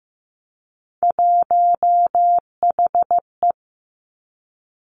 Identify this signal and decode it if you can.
Morse code, heard as a keyed tone, about 15 wpm, 700 Hz. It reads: 1HE
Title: Three Challenging Letter-Number Combinations 15wpm